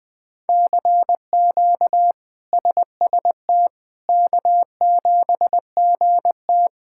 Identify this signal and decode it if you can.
Morse code, heard as a keyed tone, about 20 wpm, 700 Hz. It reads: CQ SST K7GT
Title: K1USN SST QSO Practice - 20wpm